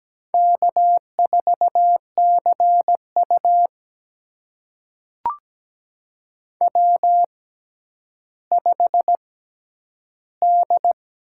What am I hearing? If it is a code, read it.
Morse code, heard as a keyed tone, about 17 wpm, spaced as for 10 wpm, 700 Hz. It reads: K4CU W5D
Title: USA Call Signs - 6x Character Spacing - 17wpm